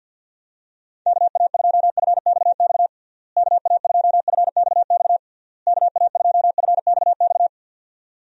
Morse code, heard as a keyed tone, about 51 wpm, 700 Hz. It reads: KA2FXX KA2FXX KA2FXX
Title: USA Call Signs - 6 Characters - Repeated 3x - 50wpm